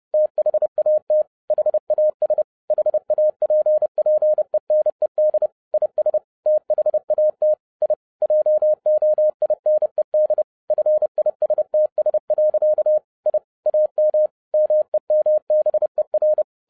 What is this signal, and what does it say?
Morse code, heard as a keyed tone, about 30 wpm, 610 Hz. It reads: THAT HAS HAPPENED IS THAT I JOINED FISTS. I AM MEMBER